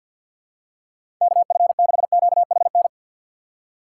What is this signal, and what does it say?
Morse code, heard as a keyed tone, about 52 wpm, 700 Hz. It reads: KF6QHN
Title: USA Call Signs - 6 Characters - 3x Character Spacing - 50wpm